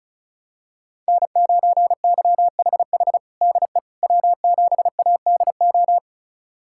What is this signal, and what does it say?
Morse code, heard as a keyed tone, about 34 wpm, 700 Hz. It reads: N9YHH DE W7ADO
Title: USA Callsign DE Callsign - 5 Characters - 35wpm